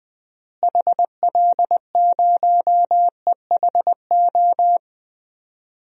Morse code, heard as a keyed tone, about 20 wpm, 700 Hz. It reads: HL0EHO
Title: DXCC-100 Call Signs - 6 Characters - Repeated 3x - 20wpm